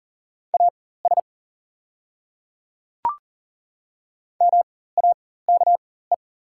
Morse code, heard as a keyed tone, about 37 wpm, spaced as for 18 wpm, 690 Hz. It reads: AS MAKE
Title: 4x Character Spacing - Top 100 Words - 40wpm